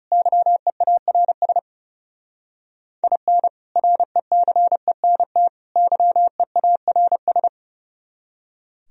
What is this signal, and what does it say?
Morse code, heard as a keyed tone, about 37 wpm, 700 Hz. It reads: YEARS IN RECENT YEARS
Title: Speed-Racing - Sets of 3 Words 30wpm